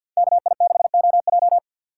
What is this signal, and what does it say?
Morse code, heard as a keyed tone, about 53 wpm, 700 Hz. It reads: KI6OJ
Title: USA Call Signs - 5 Characters - 3x Character Spacing - 50wpm